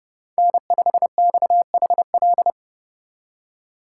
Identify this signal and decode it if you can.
Morse code, heard as a keyed tone, about 31 wpm, 700 Hz. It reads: N5XHL
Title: USA Call Signs - 5 Characters - 2x Character Spacing - 30wpm